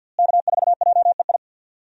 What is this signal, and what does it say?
Morse code, heard as a keyed tone, about 47 wpm, 700 Hz. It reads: K4JEI
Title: USA Call Signs - 5 Characters - Repeated 2x - 50wpm